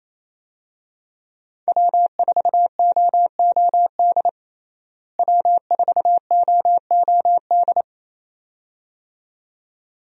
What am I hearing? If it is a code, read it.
Morse code, heard as a keyed tone, about 27 wpm, 700 Hz. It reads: W4OOD W4OOD
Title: USA Call Signs - 5 Characters - Repeated 2x - 28wpm